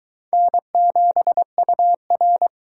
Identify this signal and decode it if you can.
Morse code, heard as a keyed tone, about 23 wpm, 700 Hz. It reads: N7UR